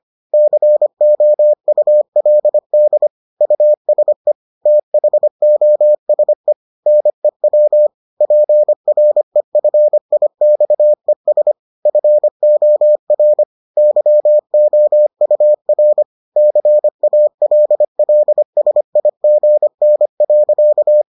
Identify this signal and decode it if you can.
Morse code, heard as a keyed tone, about 25 wpm, 600 Hz. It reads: COULD USE THOSE NEW PREFIXES FOR YOUR CALLSIGN.